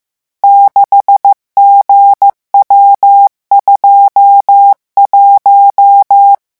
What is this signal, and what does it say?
Morse code, heard as a keyed tone, about 15 wpm, 800 Hz. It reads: 6GW21
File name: morse mp3 / revision12-vitesse 14 mots